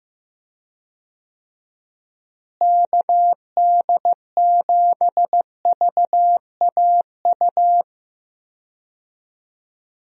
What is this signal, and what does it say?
Morse code, heard as a keyed tone, about 15 wpm, 700 Hz. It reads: KD7VAU
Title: USA Call Signs - 6 Characters - Repeated 2x - 15wpm